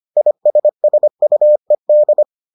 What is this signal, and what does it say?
Morse code, heard as a keyed tone, about 25 wpm, 600 Hz. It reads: ISSUED